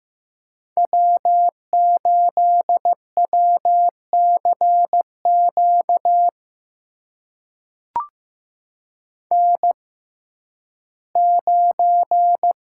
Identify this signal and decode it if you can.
Morse code, heard as a keyed tone, about 15 wpm, spaced as for 11 wpm, 700 Hz. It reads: W8WCQ N9
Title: USA Call Signs - 6x Character Spacing - 15wpm